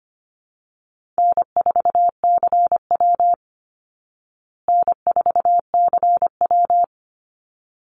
Morse code, heard as a keyed tone, about 25 wpm, 700 Hz. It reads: N4CW N4CW
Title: Easy Call Signs Repeated 2x - 25wpm